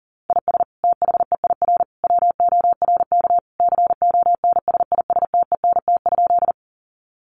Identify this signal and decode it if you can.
Morse code, heard as a keyed tone, about 40 wpm, 700 Hz. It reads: IS THEIR WORK CONSISTENT?
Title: Sentences from Top 1500 Words 40wpm